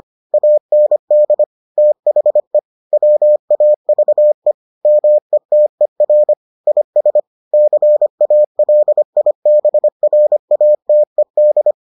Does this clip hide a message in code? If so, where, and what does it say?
Yes: Morse code from 0.3 to 11.7 s, heard as a keyed tone, about 25 wpm, 600 Hz. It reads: AND THE WAVE METER IS CALIBRATED